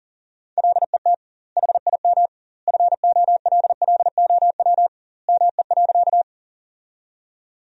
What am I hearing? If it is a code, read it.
Morse code, heard as a keyed tone, about 40 wpm, 700 Hz. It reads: LET HIM FOLLOW ME.
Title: 2x Word-Spacing - Sentences from Top 300 Words - 40wpm